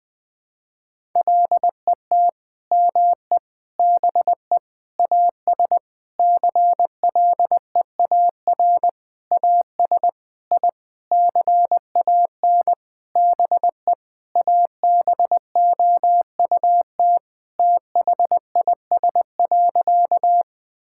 Morse code, heard as a keyed tone, about 20 wpm, 700 Hz. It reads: LET ME BE AS CLEAR AS I CAN BE ABOUT THIS.
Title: Sentences with 9-Word Phrases - 20wpm